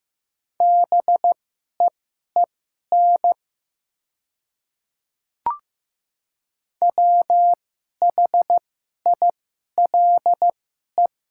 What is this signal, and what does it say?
Morse code, heard as a keyed tone, about 15 wpm, spaced as for 11 wpm, 700 Hz. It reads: BEEN WHILE
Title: 2x Character Spacing - Top 300 Words - 15wpm